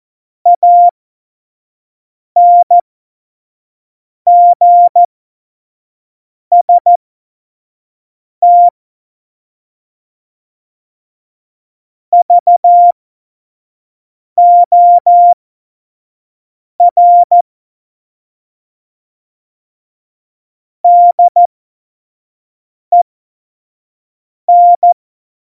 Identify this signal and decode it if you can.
Morse code, heard as a keyed tone, about 14 wpm, spaced as for 5 wpm, 700 Hz. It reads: ANGST VOR DEN
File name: Morsekurs / Witze / Witz 20. Oktober 2020, 25 F70